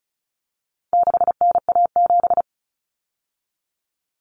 Morse code, heard as a keyed tone, about 34 wpm, 700 Hz. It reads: 6NA7
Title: Common Cut Numbers - 4-Digit 35wpm